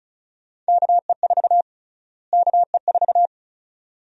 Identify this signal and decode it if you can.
Morse code, heard as a keyed tone, about 36 wpm, 700 Hz. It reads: KE4 KE4
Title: USA Call Sign Prefixes - Repeated 2x - 35wpm